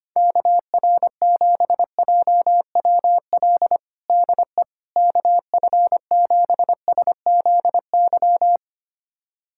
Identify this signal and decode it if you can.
Morse code, heard as a keyed tone, about 25 wpm, 700 Hz. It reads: KR7JWL DE KF7SZY
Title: USA Callsign DE Callsign - 6 Characters - 25wpm